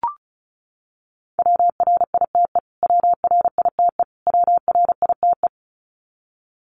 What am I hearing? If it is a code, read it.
Morse code, heard as a keyed tone, about 34 wpm, 700 Hz. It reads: WRITE WRITE WRITE
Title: Top 500 Words Repeated 35wpm